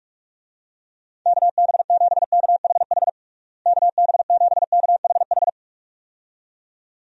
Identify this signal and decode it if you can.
Morse code, heard as a keyed tone, about 42 wpm, 700 Hz. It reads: KB7KHH KB7KHH
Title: USA Call Signs - 6 Characters - Repeated 2x - 45wpm